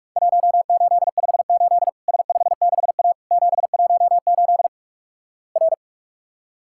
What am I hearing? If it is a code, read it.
Morse code, heard as a keyed tone, about 48 wpm, 700 Hz. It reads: 1858 S56A 718 R
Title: Worked All Europe DX Contest - QTC Exchange - 45wpm